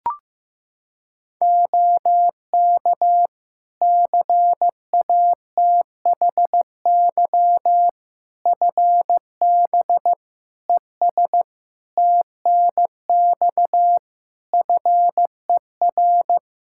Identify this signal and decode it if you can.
Morse code, heard as a keyed tone, about 15 wpm, 700 Hz. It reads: OK CATHY FB ES TNX FER
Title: K1USN SST QSO Practice 15wpm